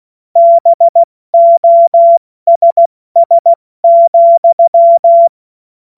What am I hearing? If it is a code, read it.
Morse code, heard as a keyed tone, about 16 wpm, 680 Hz. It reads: BOSS,
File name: Morsekurs / Witze / Witz 04. April 70 F80